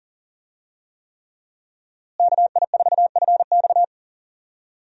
Morse code, heard as a keyed tone, about 40 wpm, 700 Hz. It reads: KI4FX
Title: USA Call Signs - 5 Characters - Repeated 3x - 40wpm